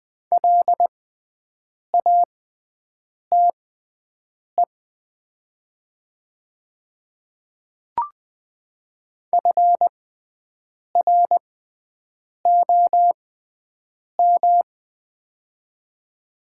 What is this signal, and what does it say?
Morse code, heard as a keyed tone, about 20 wpm, spaced as for 7 wpm, 700 Hz. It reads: LATE FROM
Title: 6x Character Spacing - Top 500 Words - 20wpm